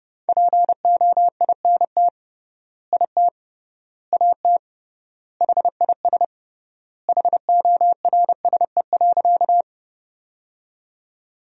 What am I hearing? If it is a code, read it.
Morse code, heard as a keyed tone, about 32 wpm, 700 Hz. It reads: POINT IT AT HIS HORSE.
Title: 3x Word-Spacing - Sentences from Top 300 Words - 30wpm